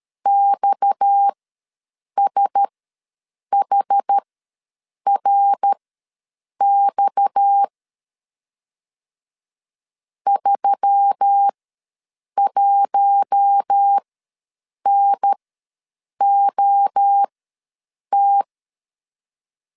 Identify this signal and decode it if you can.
Morse code, heard as a keyed tone, about 13 wpm, spaced as for 7 wpm, 800 Hz. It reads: XSHRX 31NOT
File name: LEZIONI / 6 PKX0 / 6 PKX ws 12wpm 3106